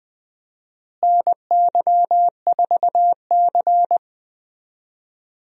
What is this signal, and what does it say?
Morse code, heard as a keyed tone, about 20 wpm, 700 Hz. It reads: NY4C
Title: USA Call Signs - 4 Characters - Repeated 3x - 20wpm